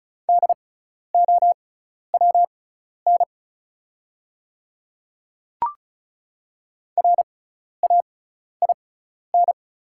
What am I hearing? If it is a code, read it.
Morse code, heard as a keyed tone, about 34 wpm, spaced as for 12 wpm, 700 Hz. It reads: DOWN RAIN
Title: 6x Character Spacing - Top 300 Words - 35wpm